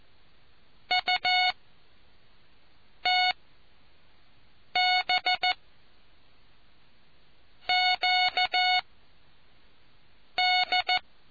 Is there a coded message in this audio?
Yes: Morse code from 0.9 to 11.0 s, heard as a keyed tone, about 14 wpm, 730 Hz. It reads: U T B Q D